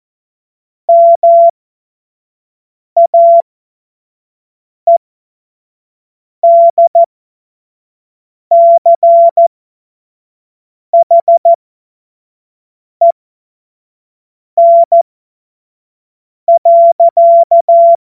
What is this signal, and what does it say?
Morse code, heard as a keyed tone, about 14 wpm, spaced as for 5 wpm, 680 Hz. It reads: MAEDCHEN.
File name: Morsekurs / Witze / Witz 13. Januar 2026 25 F70